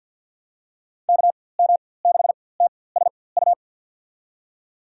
Morse code, heard as a keyed tone, about 53 wpm, spaced as for 24 wpm, 700 Hz. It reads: KM6TSU
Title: USA Call Signs - 6 Characters - 4x Character Spacing - 50wpm